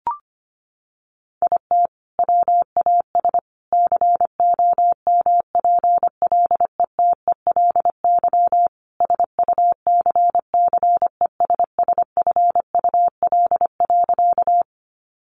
Morse code, heard as a keyed tone, about 25 wpm, 700 Hz. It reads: IT WAS COMPLETELY SUCCESSFUL.
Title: Sentences from Top 1100 Words 25wpm